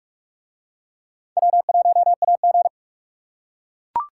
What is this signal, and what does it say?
Morse code, heard as a keyed tone, about 44 wpm, 700 Hz. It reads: W1AG